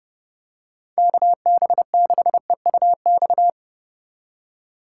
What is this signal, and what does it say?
Morse code, heard as a keyed tone, about 30 wpm, 700 Hz. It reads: KB6EUX